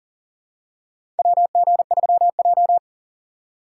Morse code, heard as a keyed tone, about 38 wpm, 700 Hz. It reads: WG3J
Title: USA Call Signs - 4 Characters - 6x Character Spacing - 40wpm